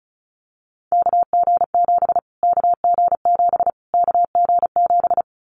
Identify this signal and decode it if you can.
Morse code, heard as a keyed tone, about 34 wpm, 700 Hz. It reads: KG7 KG7 KG7
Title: Callsign Prefixes for Top 25 DXCC - Repeated 3x - 35wpm